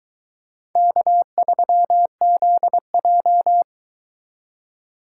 Morse code, heard as a keyed tone, about 23 wpm, 700 Hz. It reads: K3ZJ